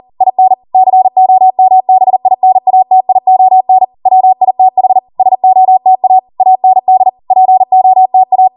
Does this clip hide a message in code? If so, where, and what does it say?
Yes: Morse code from 0.2 to 8.5 s, heard as a keyed tone, about 40 wpm, 750 Hz. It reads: IN COMBINATION WITH SOTA AND POTA